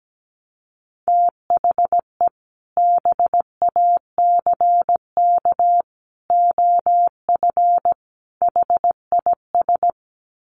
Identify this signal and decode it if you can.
Morse code, heard as a keyed tone, about 17 wpm, 700 Hz. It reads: THE BACK OF HIS
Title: Sets of 4 Words 17wpm